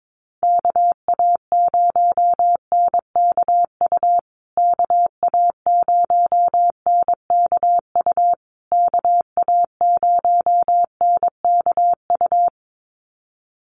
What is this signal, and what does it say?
Morse code, heard as a keyed tone, about 22 wpm, 700 Hz. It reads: KA0NKU KA0NKU KA0NKU
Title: USA Call Signs Repeated 3x 22wpm